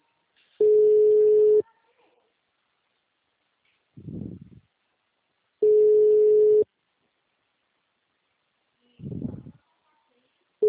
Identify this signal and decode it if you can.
Ringback tone (Europe, 425 Hz), with speech.
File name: out- -2028- - - .27957